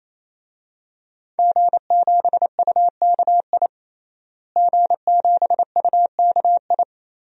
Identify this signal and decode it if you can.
Morse code, heard as a keyed tone, about 27 wpm, 700 Hz. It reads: G7UKI G7UKI
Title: DXCC-100 Call Signs - 5 Characters - Repeated 3x - 28wpm